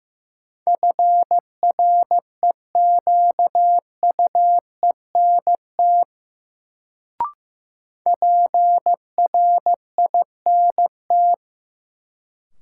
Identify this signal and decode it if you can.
Morse code, heard as a keyed tone, about 15 wpm, 700 Hz. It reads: FREQUENT PRINT
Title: Sentences from Top 1400 Words - New Word Drill Alt - 15wpm